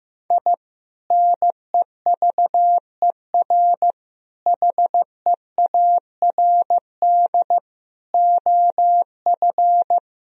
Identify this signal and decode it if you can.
Morse code, heard as a keyed tone, about 16 wpm, 700 Hz. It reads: I NEVER HEARD OF
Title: Sentences from Top 1000 Words - Review Part 1 - 15wpm